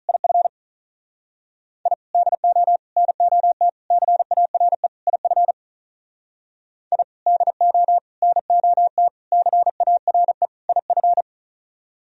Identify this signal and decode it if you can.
Morse code, heard as a keyed tone, about 46 wpm, 700 Hz. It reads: IF I DO NOT CARE IF I DO NOT CARE IF